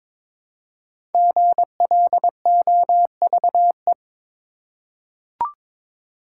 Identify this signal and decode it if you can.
Morse code, heard as a keyed tone, about 22 wpm, 700 Hz. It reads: GLOVE
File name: 5 Letter Words - First 3 Letters Given - 22wpm